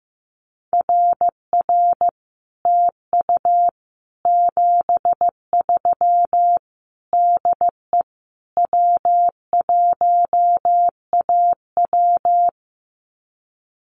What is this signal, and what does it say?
Morse code, heard as a keyed tone, about 15 wpm, 700 Hz. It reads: RR TU 73 DE W1AW
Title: Common QSO Phrases Encore 15wpm